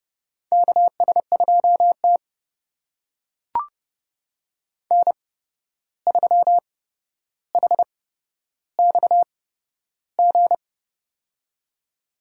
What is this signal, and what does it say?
Morse code, heard as a keyed tone, about 30 wpm, spaced as for 12 wpm, 700 Hz. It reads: KS2T N3HXG
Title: USA Call Signs - 8x Character Spacing - 30wpm